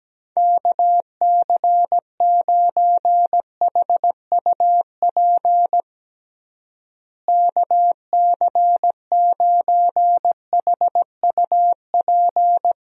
Morse code, heard as a keyed tone, about 17 wpm, 700 Hz. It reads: KC9HUP KC9HUP
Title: USA Call Signs - 6 Characters - Repeated 3x - 17wpm